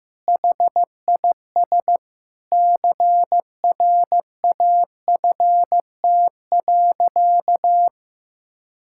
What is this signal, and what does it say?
Morse code, heard as a keyed tone, about 15 wpm, 700 Hz. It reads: HIS CRAFT.
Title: Sentences from Top 1000 Words - Review Part 6 - 15wpm